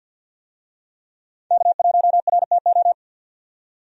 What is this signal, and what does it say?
Morse code, heard as a keyed tone, about 46 wpm, 700 Hz. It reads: K1RTO
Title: USA Call Signs - 5 Characters - 50wpm